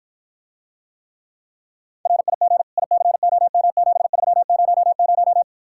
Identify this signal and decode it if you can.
Morse code, heard as a keyed tone, about 54 wpm, 700 Hz. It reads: RIG ICOM7300